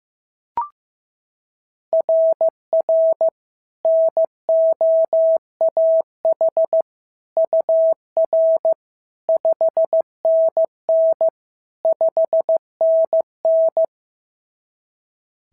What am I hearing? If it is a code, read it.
Morse code, heard as a keyed tone, about 15 wpm, 690 Hz. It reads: RR NOAH UR 5NN 5NN
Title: SOTA QSO Practice 15wpm